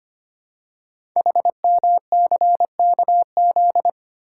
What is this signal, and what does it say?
Morse code, heard as a keyed tone, about 25 wpm, 700 Hz. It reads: HMCKZ
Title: Code Groups - 5 Letters - 25wpm